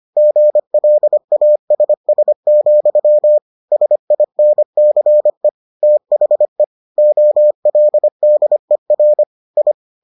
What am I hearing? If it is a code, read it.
Morse code, heard as a keyed tone, about 25 wpm, 600 Hz. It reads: GLASS, SINCE THE OLDER I